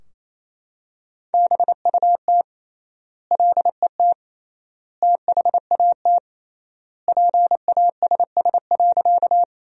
Morse code, heard as a keyed tone, about 28 wpm, 700 Hz. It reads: BUT LET THAT PASS.
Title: 3x Word-Spacing - Sentences from Top 400 Words - 28wpm